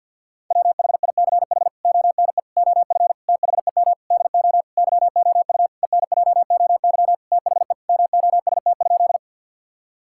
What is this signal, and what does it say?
Morse code, heard as a keyed tone, about 50 wpm, 700 Hz. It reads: WHICH ONE OF THEM DO YOU ENJOY THE MOST?
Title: Sentences from Top 1000 Words - Review Part 5 - 50wpm